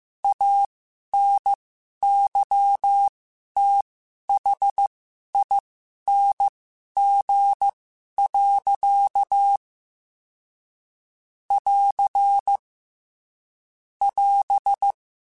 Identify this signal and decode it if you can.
Morse code, heard as a keyed tone, about 15 wpm, spaced as for 11 wpm, 800 Hz. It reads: ANYTHING. <AR> <AS>
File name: Quotes / Quotes-mp3s / StandFor @13